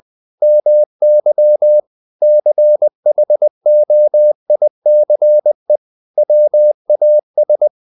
Morse code, heard as a keyed tone, about 20 wpm, 600 Hz. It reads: MY CHOICE WAS
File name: the old man and the sea / the old man and the sea part 1 20wpm